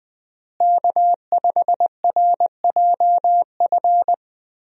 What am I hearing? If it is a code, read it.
Morse code, heard as a keyed tone, about 20 wpm, 700 Hz. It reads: K5RJF
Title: USA Call Signs - 5 Characters - Repeated 2x - 20wpm